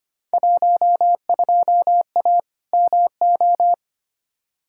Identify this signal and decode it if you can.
Morse code, heard as a keyed tone, about 25 wpm, 700 Hz. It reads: 12A MO
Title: Field Day Exchanges Encore - 25wpm